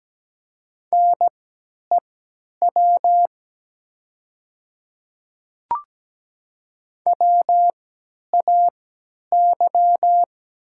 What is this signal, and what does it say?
Morse code, heard as a keyed tone, about 17 wpm, spaced as for 10 wpm, 700 Hz. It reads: NEW WAY
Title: 3x Character Spacing - Top 300 Words - 17wpm